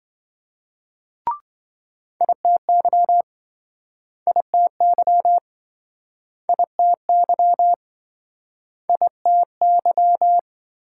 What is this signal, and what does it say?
Morse code, heard as a keyed tone, about 29 wpm, 700 Hz. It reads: ITY ITY ITY ITY